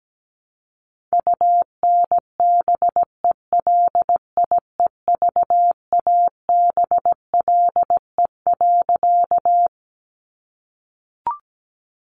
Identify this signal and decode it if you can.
Morse code, heard as a keyed tone, about 17 wpm, 700 Hz. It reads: UNBELIEVABLE.
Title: A Sound Of Thunder 17wpm